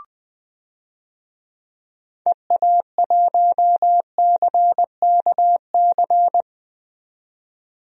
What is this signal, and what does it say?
Morse code, heard as a keyed tone, about 20 wpm, 700 Hz. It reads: EA1CKC
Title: DXCC-100 Call Signs - 6 Characters - 20wpm